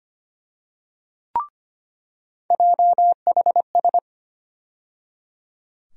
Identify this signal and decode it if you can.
Morse code, heard as a keyed tone, about 25 wpm, 700 Hz. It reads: JHS
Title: USA Call Sign Postfixes - 3 Letters 25wpm